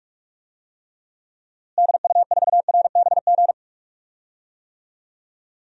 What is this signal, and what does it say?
Morse code, heard as a keyed tone, about 42 wpm, 700 Hz. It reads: DU4RBG
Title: DXCC-100 Call Signs - 6 Characters - Repeated 3x - 45wpm